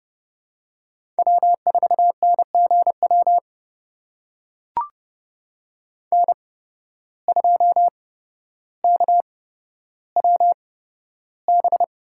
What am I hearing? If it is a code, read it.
Morse code, heard as a keyed tone, about 30 wpm, spaced as for 13 wpm, 700 Hz. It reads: W4NGW N2KWB
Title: USA Call Signs - 5 Characters - 8x Character Spacing - 30wpm